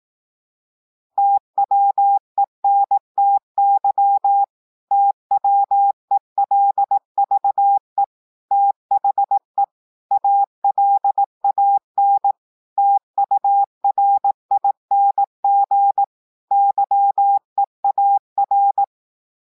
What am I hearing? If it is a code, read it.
Morse code, heard as a keyed tone, about 18 wpm, 800 Hz. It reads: TWENTY TWELVE THE ALAN TURING YEAR